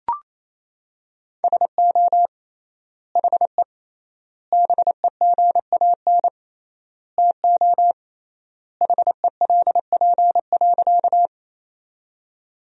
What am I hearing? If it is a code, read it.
Morse code, heard as a keyed tone, about 28 wpm, 700 Hz. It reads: SO HE BEGAN TO HELP.
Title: 3x Word-Spacing - Sentences from Top 300 Words - 28wpm